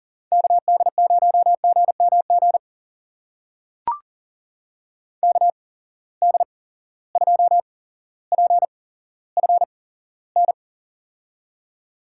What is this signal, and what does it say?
Morse code, heard as a keyed tone, about 37 wpm, spaced as for 17 wpm, 700 Hz. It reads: KD0GMG KD2PFN
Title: USA Call Signs - 6 Characters - 8x Character Spacing - 40wpm